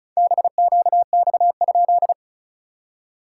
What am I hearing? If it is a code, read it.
Morse code, heard as a keyed tone, about 35 wpm, 700 Hz. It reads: BQX?
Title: SOTA QSO Practice 35wpm